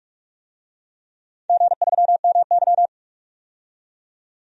Morse code, heard as a keyed tone, about 43 wpm, 700 Hz. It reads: G3MY
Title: DXCC-100 Call Signs - 4 Characters - 45wpm